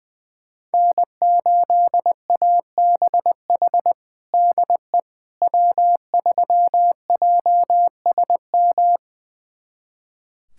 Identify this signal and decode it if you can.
Morse code, heard as a keyed tone, about 20 wpm, 700 Hz. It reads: N8ABH DE W3JSM
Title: USA Callsign DE Callsign - 5 Characters - 20wpm